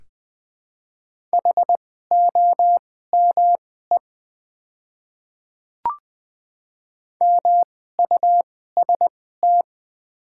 Morse code, heard as a keyed tone, about 20 wpm, spaced as for 14 wpm, 700 Hz. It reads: HOME MUST
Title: 2x Character Spacing - Top 300 Words - 20wpm